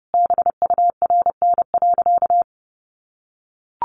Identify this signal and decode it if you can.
Morse code, heard as a keyed tone, about 30 wpm, 700 Hz. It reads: BURN.
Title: Sentences from Top 1300 Words 30wpm